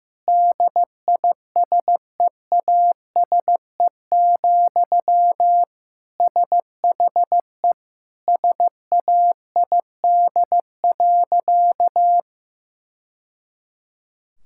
Morse code, heard as a keyed tone, about 16 wpm, 700 Hz. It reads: DISEASE, SHE SAID.
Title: Sentences from Top 1000 Words - Review Part 6 - 15wpm